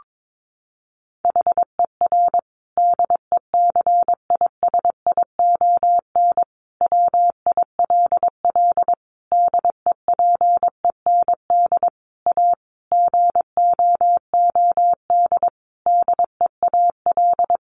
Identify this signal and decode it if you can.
Morse code, heard as a keyed tone, about 22 wpm, 700 Hz. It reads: HER DECISION WILL DEPEND A GOOD DEAL
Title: Sentences from Top 1300 Words 22wpm